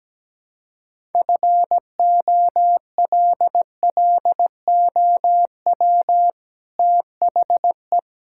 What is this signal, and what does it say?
Morse code, heard as a keyed tone, about 17 wpm, 700 Hz. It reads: FOLLOW THE
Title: Sentences from Top 1000 Words - Review Part 6 - 17wpm